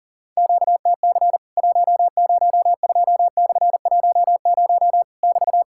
Morse code, heard as a keyed tone, about 40 wpm, 700 Hz. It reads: QTC 102/10 <BT>
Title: Worked All Europe DX Contest - QTC Exchange - 40wpm